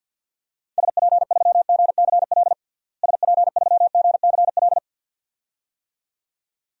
Morse code, heard as a keyed tone, about 51 wpm, 700 Hz. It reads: SP3GCL SP3GCL
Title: DXCC-100 Call Signs - 6 Characters - Repeated 2x - 50wpm